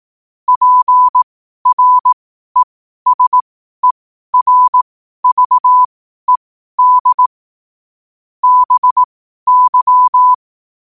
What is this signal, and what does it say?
Morse code, heard as a keyed tone, about 18 wpm, spaced as for 12 wpm, 1000 Hz. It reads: PRESERVED BY
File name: qotdfiles / 12wpm